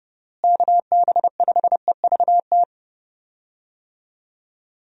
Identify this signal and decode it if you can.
Morse code, heard as a keyed tone, about 30 wpm, 700 Hz. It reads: KB5EVT
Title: USA Call Signs - 6 Characters - Repeated 3x - 30wpm